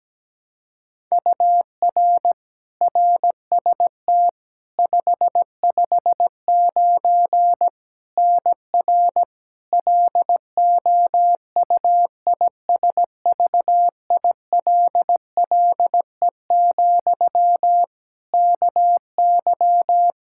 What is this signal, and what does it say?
Morse code, heard as a keyed tone, about 17 wpm, 700 Hz. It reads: UR RST 559 NR LOUISVILLE, KY
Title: K1USN SST QSO Practice 17wpm